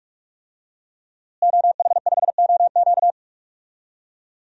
Morse code, heard as a keyed tone, about 46 wpm, 700 Hz. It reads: OH5OQ
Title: DXCC-100 Call Signs - 5 Characters - Repeated 3x - 45wpm